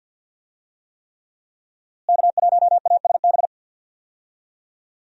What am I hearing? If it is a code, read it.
Morse code, heard as a keyed tone, about 52 wpm, 700 Hz. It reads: K1ASB
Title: USA Call Signs - 5 Characters - Repeated 3x - 50wpm